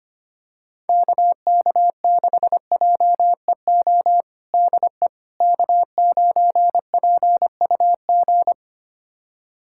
Morse code, heard as a keyed tone, about 25 wpm, 700 Hz. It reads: KK6JEO DE K9PUG
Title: USA Callsign DE Callsign 25wpm